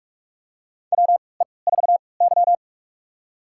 Morse code, heard as a keyed tone, about 46 wpm, spaced as for 25 wpm, 700 Hz. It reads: WE4Y
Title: USA Call Signs - 4 Characters - 3x Character Spacing - 45wpm